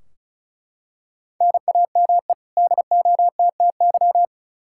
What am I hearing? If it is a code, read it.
Morse code, heard as a keyed tone, about 34 wpm, 700 Hz. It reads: NAME DOTTY
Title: K1USN SST QSO Practice 35wpm